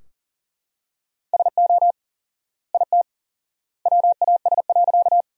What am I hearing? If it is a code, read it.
Morse code, heard as a keyed tone, about 40 wpm, 700 Hz. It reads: SO IT WAS.
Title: 4x Word-Spacing - Sentences from Top 100 Words - 40wpm